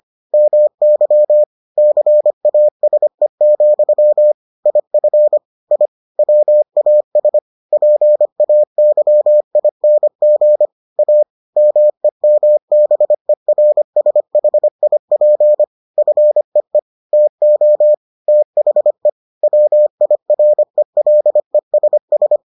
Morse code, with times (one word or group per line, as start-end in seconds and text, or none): 0.3-1.4 MY
1.8-4.3 CASE,
4.7-5.4 IF
5.7-5.9 I
6.2-7.4 WAS
7.7-10.7 PAYING
11.0-11.2 A
11.6-15.6 MEMBERSHIP
16.0-16.8 FEE
17.1-17.9 TO
18.3-19.1 THE
19.4-22.4 WIRELESS